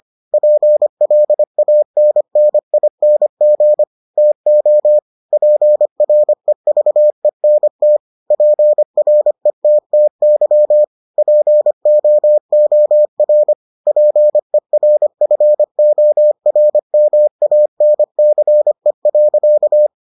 Morse code, heard as a keyed tone, about 25 wpm, 600 Hz. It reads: PLANNING TO PREVENT PRETTY POOR PERFORMANCE.